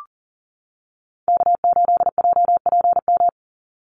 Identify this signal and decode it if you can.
Morse code, heard as a keyed tone, about 40 wpm, 700 Hz. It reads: K8JPM
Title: Call Signs - 40wpm